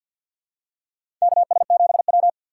Morse code, heard as a keyed tone, about 52 wpm, 700 Hz. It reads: KS7W
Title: USA Call Signs - 6x Character Spacing - 50wpm